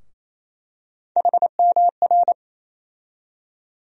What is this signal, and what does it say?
Morse code, heard as a keyed tone, about 28 wpm, 700 Hz. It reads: HMR
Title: USA Call Sign Postfixes - 3 Letters 28wpm